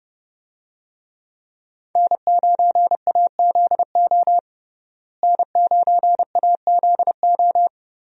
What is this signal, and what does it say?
Morse code, heard as a keyed tone, about 30 wpm, 700 Hz. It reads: N9AZO N9AZO
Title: USA Call Signs - 5 Characters - Repeated 2x - 30wpm